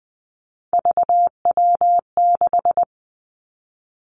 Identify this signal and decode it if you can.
Morse code, heard as a keyed tone, about 20 wpm, 700 Hz. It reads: VW6
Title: Three Challenging Letter-Number Combinations 20wpm